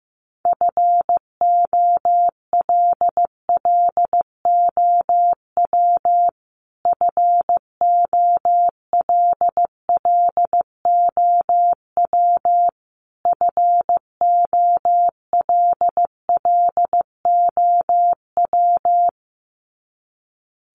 Morse code, heard as a keyed tone, about 15 wpm, 700 Hz. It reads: FOLLOW FOLLOW FOLLOW
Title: Top 500 Words Repeated 15wpm